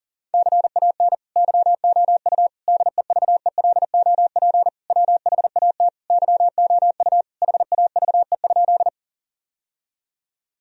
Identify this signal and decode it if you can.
Morse code, heard as a keyed tone, about 40 wpm, 700 Hz. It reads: CAN YOU DEVELOP WHAT YOU HAVE?
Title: Sentences from Top 1000 Words - Review Part 2 - 40wpm